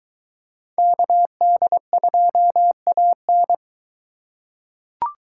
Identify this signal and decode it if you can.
Morse code, heard as a keyed tone, about 23 wpm, 700 Hz. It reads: KD2AN